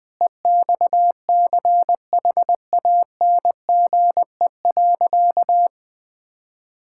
Morse code, heard as a keyed tone, about 20 wpm, 700 Hz. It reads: EXCHANGE.
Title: Sentences from Top 1000 Words - Review Part 6 - 20wpm